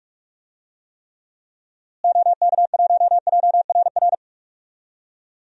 Morse code, heard as a keyed tone, about 41 wpm, 700 Hz. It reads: OK1JRR
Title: DXCC-100 Call Signs - 6 Characters - Repeated 3x - 45wpm